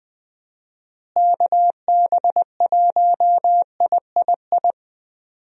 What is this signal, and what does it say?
Morse code, heard as a keyed tone, about 20 wpm, 700 Hz. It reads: KB1III
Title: USA Call Signs - 6 Characters - Repeated 3x - 20wpm